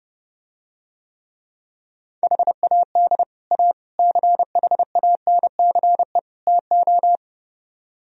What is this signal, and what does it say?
Morse code, heard as a keyed tone, about 30 wpm, 700 Hz. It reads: HAD A CHANCE TO